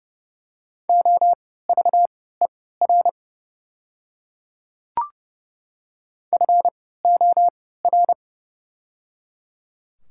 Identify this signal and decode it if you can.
Morse code, heard as a keyed tone, about 30 wpm, spaced as for 17 wpm, 700 Hz. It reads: OVER FOR
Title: 3x Character Spacing - Top 100 Words - 30wpm